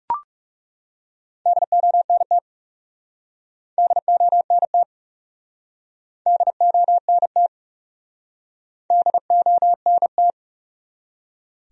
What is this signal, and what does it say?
Morse code, heard as a keyed tone, about 40 wpm, 700 Hz. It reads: DONT DONT DONT DONT